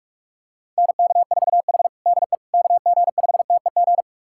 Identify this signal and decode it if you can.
Morse code, heard as a keyed tone, about 46 wpm, 690 Hz. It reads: NK4H DE KG5TEG
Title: USA Callsign DE Callsign 45wpm